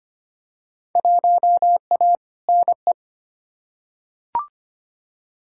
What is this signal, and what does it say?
Morse code, heard as a keyed tone, about 25 wpm, 700 Hz. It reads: 1A NE
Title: Field Day Exchanges Encore - 25wpm